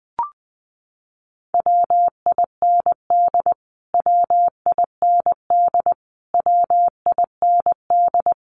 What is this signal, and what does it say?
Morse code, heard as a keyed tone, about 19 wpm, 700 Hz. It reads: WIND WIND WIND
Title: Top 500 Words Repeated 20wpm